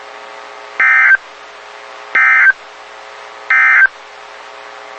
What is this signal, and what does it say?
Emergency Alert System (SAME) tones, heard as harsh buzzing bursts. It is heard from 0.7 to 4.0 s.
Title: WXK91 NOAA Weather Radio